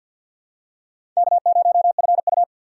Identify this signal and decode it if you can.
Morse code, heard as a keyed tone, about 48 wpm, 700 Hz. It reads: K0FU
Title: USA Call Signs - 4 Characters - 3x Character Spacing - 50wpm